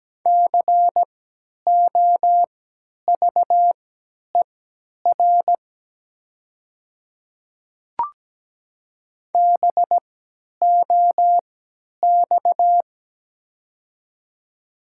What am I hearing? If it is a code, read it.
Morse code, heard as a keyed tone, about 17 wpm, spaced as for 10 wpm, 700 Hz. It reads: COVER BOX
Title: 3x Character Spacing - Top 500 Words - 17wpm